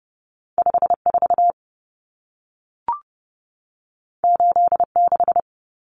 Morse code, heard as a keyed tone, about 30 wpm, 700 Hz. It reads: 54 86
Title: Numbers - 2-Digit 30wpm